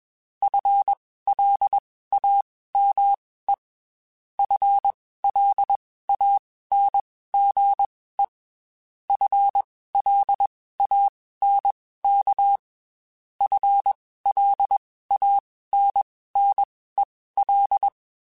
Morse code, heard as a keyed tone, about 21 wpm, spaced as for 15 wpm, 800 Hz. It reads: FLAME FLANGE FLANK FLANNEL